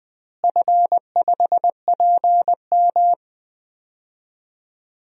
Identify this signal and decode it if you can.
Morse code, heard as a keyed tone, about 20 wpm, 700 Hz. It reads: F5PM
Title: DXCC-100 Call Signs - 4 Characters - Repeated 3x - 20wpm